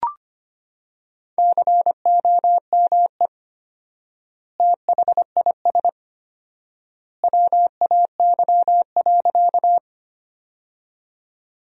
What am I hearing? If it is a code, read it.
Morse code, heard as a keyed tone, about 25 wpm, 700 Hz. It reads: COME THIS WAY.
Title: 4x Word-Spacing - Sentences from Top 100 Words - 25wpm